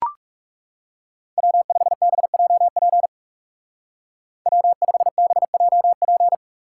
Morse code, heard as a keyed tone, about 37 wpm, 700 Hz. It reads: W5BJP W5BJP
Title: Speed Racing - USA 1x3 Call Signs 30wpm